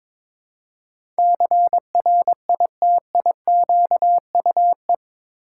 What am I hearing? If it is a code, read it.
Morse code, heard as a keyed tone, about 22 wpm, 700 Hz. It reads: CRITIQUE
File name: practice / 22wpm / Eight Letter Words - 22wpm